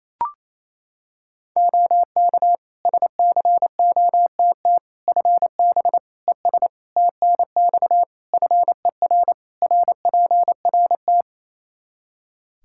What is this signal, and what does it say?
Morse code, heard as a keyed tone, about 28 wpm, 700 Hz. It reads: OK SCOTT FB ES TNX FER RPRT
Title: Practice QSO - Exchange 2 - Info - 28wpm